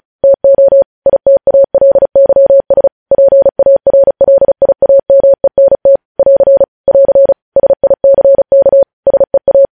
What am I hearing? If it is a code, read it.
Morse code, heard as a keyed tone, about 36 wpm, 550 Hz. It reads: TO ITALYS PARLIAMENT <AR> <AR> SICK SEA
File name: podcast / News headlines-1x-35x35